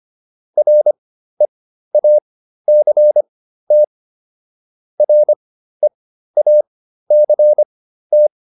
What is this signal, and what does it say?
Morse code, heard as a keyed tone, about 25 wpm, spaced as for 13 wpm, 600 Hz. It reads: REACT REACT